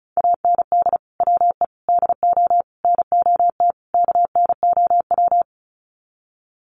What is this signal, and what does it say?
Morse code, heard as a keyed tone, about 34 wpm, 700 Hz. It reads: AND WE DO NOT KNOW
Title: Sets of 5 Words Encore 35wpm